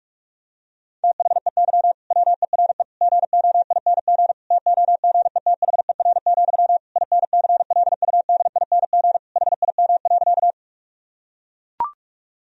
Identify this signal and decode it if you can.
Morse code, heard as a keyed tone, about 41 wpm, 700 Hz. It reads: THEY WERE GOING TOGETHER, INCLUDING HIM.
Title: Sentences from Top 1000 Words - Review Part 4 - 45wpm